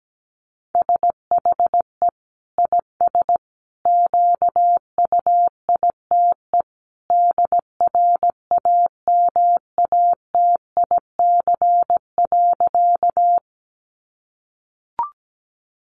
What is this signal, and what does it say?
Morse code, heard as a keyed tone, about 17 wpm, 700 Hz. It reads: SHE IS QUITE DRAMATIC.
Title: Sentences from Top 1300 Words 17wpm